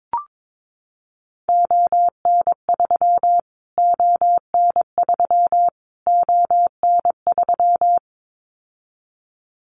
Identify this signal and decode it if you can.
Morse code, heard as a keyed tone, about 22 wpm, 700 Hz. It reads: ON3 ON3 ON3
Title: Callsign Prefixes for Top 25 DXCC - Repeated 3x - 22wpm